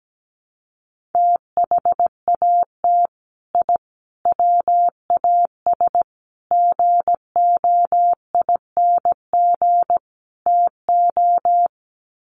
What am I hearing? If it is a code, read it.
Morse code, heard as a keyed tone, about 17 wpm, 700 Hz. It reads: THAT I WAS GOING TO
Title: Sets of 5 Words Encore 17wpm